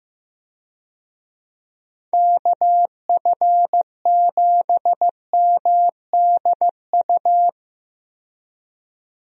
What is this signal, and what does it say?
Morse code, heard as a keyed tone, about 15 wpm, 700 Hz. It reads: KF7MDU
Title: USA Call Signs - 6 Characters - 15wpm